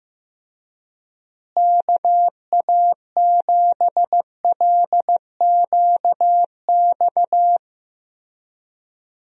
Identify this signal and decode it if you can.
Morse code, heard as a keyed tone, about 15 wpm, 700 Hz. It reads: KA7LQX
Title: USA Call Signs - 6 Characters - Repeated 2x - 15wpm